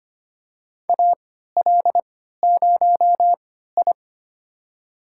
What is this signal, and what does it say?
Morse code, heard as a keyed tone, about 25 wpm, spaced as for 14 wpm, 700 Hz. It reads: AL0I
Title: USA Call Signs - 4 Characters - 3x Character Spacing - 25wpm